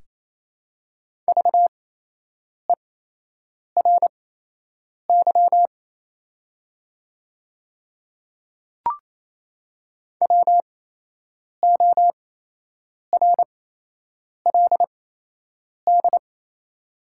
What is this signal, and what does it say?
Morse code, heard as a keyed tone, about 28 wpm, spaced as for 8 wpm, 700 Hz. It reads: VERY WORLD
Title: 8x Character Spacing - Top 200 Words - 28wpm